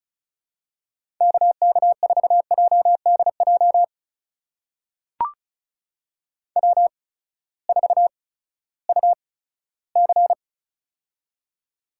Morse code, heard as a keyed tone, about 36 wpm, spaced as for 18 wpm, 700 Hz. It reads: KK4JDJ W4UC
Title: USA Call Signs - 8x Character Spacing - 35wpm